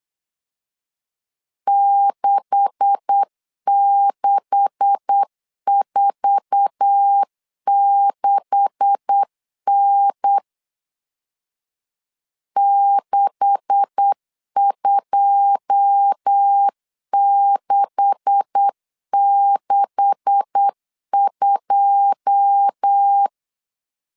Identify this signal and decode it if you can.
Morse code, heard as a keyed tone, about 8 wpm, 800 Hz. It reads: 6646N 62662
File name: LEZIONI / 9 286 8wpm 3425